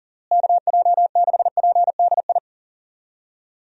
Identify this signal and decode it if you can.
Morse code, heard as a keyed tone, about 40 wpm, 700 Hz. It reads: KJ6PDI
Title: USA Call Signs - 6 Characters - Repeated 3x - 40wpm